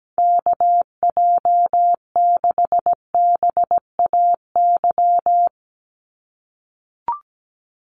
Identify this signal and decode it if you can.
Morse code, heard as a keyed tone, about 17 wpm, 700 Hz. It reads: KJ6BAY
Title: USA Call Signs 17wpm